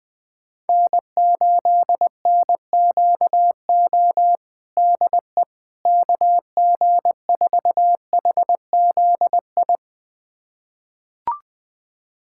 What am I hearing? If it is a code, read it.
Morse code, heard as a keyed tone, about 21 wpm, 700 Hz. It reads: N8NQO DE KG4HZI
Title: USA Callsign DE Callsign 20wpm